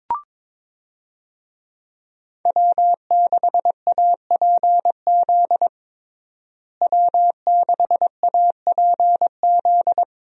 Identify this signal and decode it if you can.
Morse code, heard as a keyed tone, about 21 wpm, 700 Hz. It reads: W6APZ W6APZ
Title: USA Call Signs - 5 Characters - Repeated 3x - 22wpm